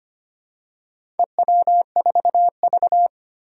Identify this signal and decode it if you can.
Morse code, heard as a keyed tone, about 25 wpm, 700 Hz. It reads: EW4V
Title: Four Challenging Letter-Number Combinations 25wpm